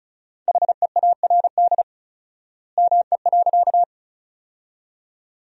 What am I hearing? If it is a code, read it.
Morse code, heard as a keyed tone, about 36 wpm, 700 Hz. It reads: HEARD ME.
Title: 4x Word-Spacing - Sentences from Top 400 Words - 35wpm